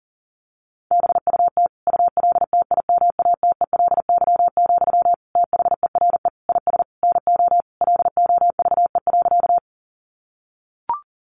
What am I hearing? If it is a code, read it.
Morse code, heard as a keyed tone, about 40 wpm, 700 Hz. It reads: BUT ULTIMATELY, THERE IS NO LOVE.
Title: Sentences from Top 1200 Words 40wpm